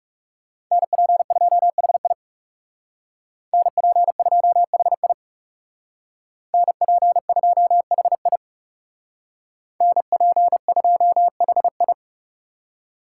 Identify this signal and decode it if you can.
Morse code, heard as a keyed tone, about 38 wpm, 700 Hz. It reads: NP2HI NP2HI NP2HI NP2HI